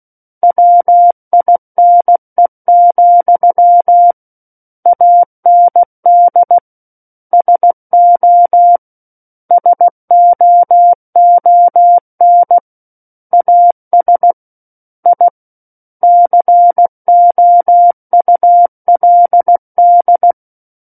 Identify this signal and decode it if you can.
Morse code, heard as a keyed tone, about 16 wpm, 700 Hz. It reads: WINE, AND SO SOON AS I COULD
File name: Book One / 07-Chapter 7 - 16 WPM